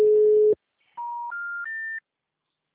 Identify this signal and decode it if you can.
Special information tone.